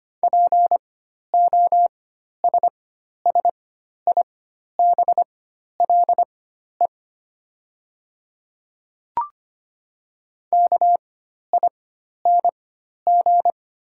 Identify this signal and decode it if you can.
Morse code, heard as a keyed tone, about 25 wpm, spaced as for 12 wpm, 690 Hz. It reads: POSSIBLE KING
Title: 4x Character Spacing - Top 500 Words - 25wpm